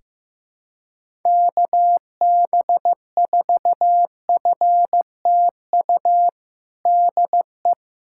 Morse code, heard as a keyed tone, about 15 wpm, 700 Hz. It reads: KB4FTU DE
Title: USA Callsign DE Callsign - 6 Characters - 15wpm